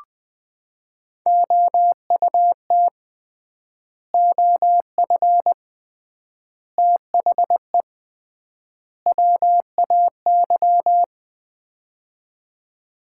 Morse code, heard as a keyed tone, about 21 wpm, 700 Hz. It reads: OUT OF THE WAY
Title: 3x Word-Spacing - Sets of 4 Words - 20wpm